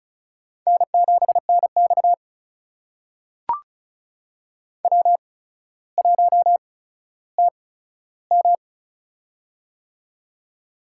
Morse code, heard as a keyed tone, about 36 wpm, spaced as for 15 wpm, 700 Hz. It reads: N7NX W1TM
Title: USA Call Signs - 4 Characters - 8x Character Spacing - 35wpm